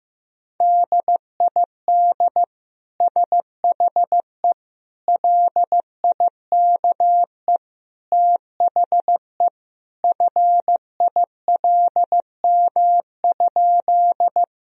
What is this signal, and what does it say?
Morse code, heard as a keyed tone, about 15 wpm, 700 Hz. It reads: DID SHE LIKE THE FILM?
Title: Sentences from Top 1000 Words - Review Part 3 - 15wpm